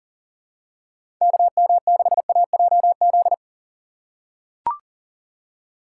Morse code, heard as a keyed tone, about 40 wpm, 700 Hz. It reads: KM6AJZ
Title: USA Call Signs - 2x Character Spacing - 40wpm